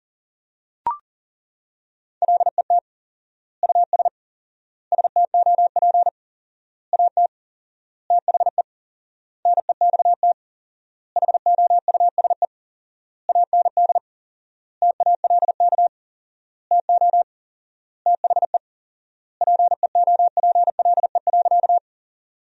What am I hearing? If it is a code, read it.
Morse code, heard as a keyed tone, about 37 wpm, 700 Hz. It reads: LET US STOP AT THE NEXT HOUSE AND TALK TO THE PEOPLE.
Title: 4x Word-Spacing - Sentences from Top 400 Words - 40wpm